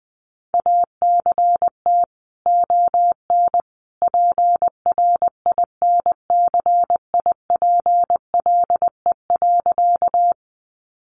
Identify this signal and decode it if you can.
Morse code, heard as a keyed tone, about 20 wpm, 700 Hz. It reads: ACT ON PRINCIPLE.
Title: Sentences from Top 1400 Words 20wpm